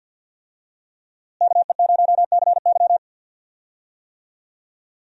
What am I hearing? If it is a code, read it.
Morse code, heard as a keyed tone, about 51 wpm, 700 Hz. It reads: KE0CY
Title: USA Call Signs - 5 Characters - Repeated 2x - 50wpm